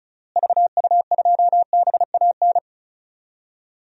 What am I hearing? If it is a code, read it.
Morse code, heard as a keyed tone, about 36 wpm, 700 Hz. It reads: VU2BAN
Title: DXCC-100 Call Signs - 6 Characters - Repeated 3x - 35wpm